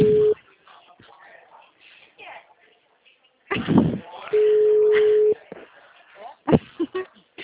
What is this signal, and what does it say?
Ringback tone (Europe, 425 Hz), with speech.